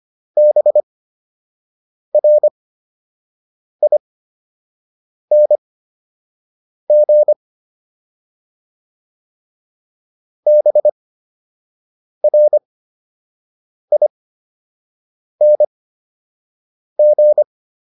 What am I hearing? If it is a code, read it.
Morse code, heard as a keyed tone, about 25 wpm, spaced as for 6 wpm, 600 Hz. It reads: BRING BRING